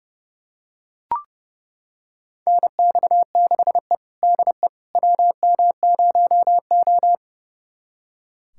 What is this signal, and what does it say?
Morse code, heard as a keyed tone, about 32 wpm, 700 Hz. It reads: NX6E DE WM0O
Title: USA Callsign DE Callsign - 4 Characters - 30wpm